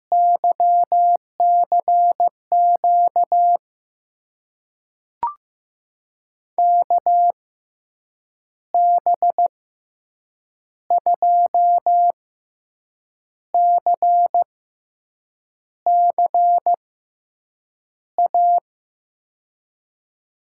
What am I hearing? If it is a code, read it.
Morse code, heard as a keyed tone, about 15 wpm, spaced as for 6 wpm, 700 Hz. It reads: YCQ KB2CCA
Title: USA Call Signs - 6 Characters - 6x Character Spacing - 15wpm